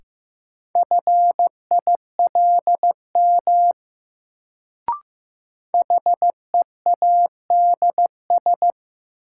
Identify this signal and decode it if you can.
Morse code, heard as a keyed tone, about 15 wpm, 700 Hz. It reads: FILM HEADS
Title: Sentences from Top 1000 Words - Review Part 3 - New Word Drill - 15wpm